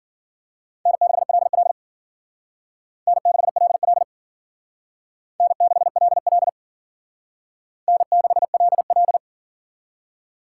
Morse code, heard as a keyed tone, about 53 wpm, 700 Hz. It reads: N6LL N6LL N6LL N6LL